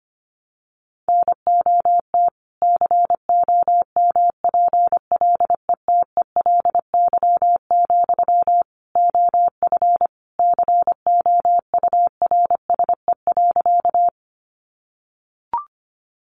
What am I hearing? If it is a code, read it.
Morse code, heard as a keyed tone, about 25 wpm, 700 Hz. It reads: NOT COMPLETELY, OF COURSE.
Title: Sentences from Top 1100 Words 25wpm